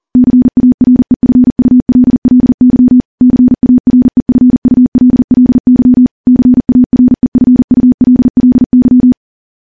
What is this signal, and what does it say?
Morse code, heard as a keyed tone, about 40 wpm, 250 Hz. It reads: CAREFULLY CAREFULLY CAREFULLY
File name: morse / morse-training-40WPM